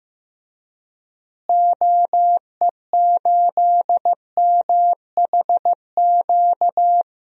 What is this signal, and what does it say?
Morse code, heard as a keyed tone, about 16 wpm, 700 Hz. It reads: OE8MHQ
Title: DXCC-100 Call Signs - 6 Characters - Repeated 2x - 15wpm